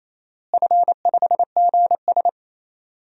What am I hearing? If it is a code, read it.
Morse code, heard as a keyed tone, about 28 wpm, 700 Hz. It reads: F5GS
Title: DXCC-100 Call Signs - 4 Characters - Repeated 2x - 28wpm